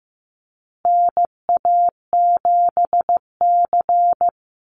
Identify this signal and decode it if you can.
Morse code, heard as a keyed tone, about 14 wpm, 700 Hz. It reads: NA7C
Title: Call Signs Repeated 2x - 15wpm